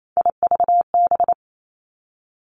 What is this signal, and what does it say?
Morse code, heard as a keyed tone, about 28 wpm, 700 Hz. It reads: IVB
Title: Three Challenging Letter-Number Combinations 28wpm